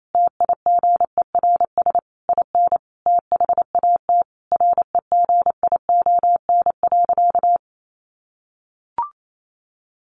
Morse code, heard as a keyed tone, about 28 wpm, 700 Hz. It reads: TIGERS IN THAT REGION.
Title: A Sound Of Thunder 28wpm